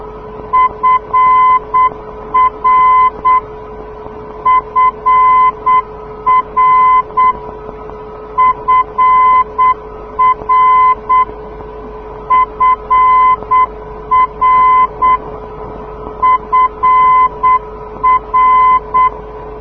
Morse code, heard as a keyed tone, about 8 wpm, 1020 Hz. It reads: FR FR FR FR FR
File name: europe / d-6028 - ndb frankfurt fr-297 (2004) (rainer brannolte, d) xc